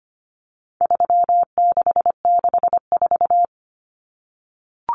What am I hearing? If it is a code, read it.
Morse code, heard as a keyed tone, about 25 wpm, 700 Hz. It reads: 3664
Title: Common Cut Numbers - 4-Digit 25wpm